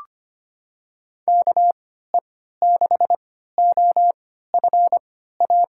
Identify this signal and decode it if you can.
Morse code, heard as a keyed tone, about 25 wpm, spaced as for 14 wpm, 700 Hz. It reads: KE6OFA
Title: USA Call Signs - 6 Characters - 3x Character Spacing - 25wpm